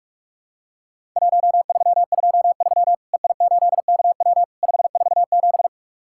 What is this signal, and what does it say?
Morse code, heard as a keyed tone, about 44 wpm, 700 Hz. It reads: 1323 EI8KW 547
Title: Worked All Europe DX Contest - QTC Exchange - 45wpm